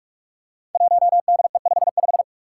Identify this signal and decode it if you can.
Morse code, heard as a keyed tone, about 44 wpm, 700 Hz. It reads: 1DE55
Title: Five Challenging Letter-Number Combinations 45wpm